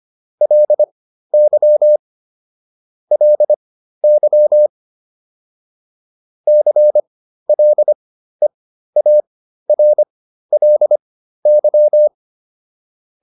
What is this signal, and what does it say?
Morse code, heard as a keyed tone, about 25 wpm, spaced as for 13 wpm, 600 Hz. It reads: LY LY CLEARLY
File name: LY7 13